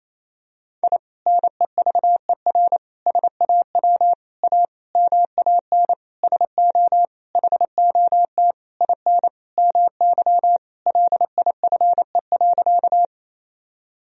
Morse code, heard as a keyed tone, about 27 wpm, 700 Hz. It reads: I NEVER SAW A MAN SO HOT IN MY LIFE.
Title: Sentences from Top 1000 Words - Review Part 3 - 28wpm